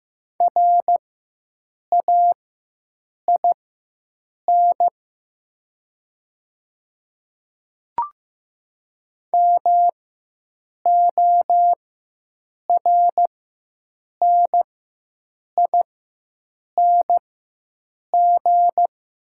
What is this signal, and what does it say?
Morse code, heard as a keyed tone, about 15 wpm, spaced as for 7 wpm, 700 Hz. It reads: RAIN MORNING
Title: 4x Character Spacing - Top 500 Words - 15wpm